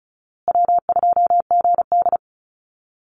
Morse code, heard as a keyed tone, about 34 wpm, 700 Hz. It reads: W2GD
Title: Easy Call Signs Repeated 3x - 35wpm